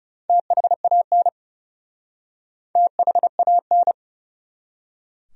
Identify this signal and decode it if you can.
Morse code, heard as a keyed tone, about 40 wpm, 700 Hz. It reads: THAN THAN